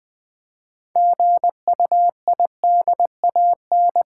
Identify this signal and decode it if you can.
Morse code, heard as a keyed tone, about 20 wpm, 700 Hz. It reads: GUIDAN
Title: Build A Word - 3 Letters At A Time - 8-Letter Words - 20wpm